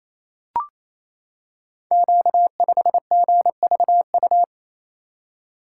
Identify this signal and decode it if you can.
Morse code, heard as a keyed tone, about 28 wpm, 700 Hz. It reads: Q5GVU
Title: Code Groups - 5 Letter:Numbers - 28wpm